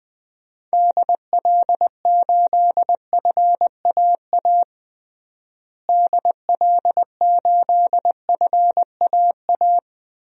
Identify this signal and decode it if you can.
Morse code, heard as a keyed tone, about 20 wpm, 700 Hz. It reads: DL8FAA DL8FAA
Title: DXCC-100 Call Signs - 6 Characters - Repeated 2x - 20wpm